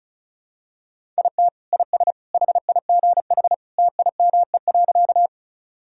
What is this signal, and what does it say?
Morse code, heard as a keyed tone, about 36 wpm, 700 Hz. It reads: IT IS HIGH TIME.
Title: Sentences from Top 1000 Words - Review Part 1 - 35wpm